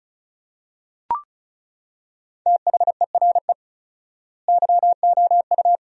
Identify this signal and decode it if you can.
Morse code, heard as a keyed tone, about 34 wpm, 700 Hz. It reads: THERE YOU
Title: 4x Word-Spacing - Sentences from Top 100 Words - 35wpm